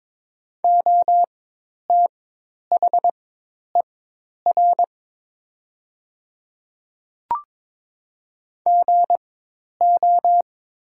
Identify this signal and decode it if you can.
Morse code, heard as a keyed tone, about 22 wpm, spaced as for 10 wpm, 700 Hz. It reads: OTHER GO
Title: 4x Character Spacing - Top 300 Words - 22wpm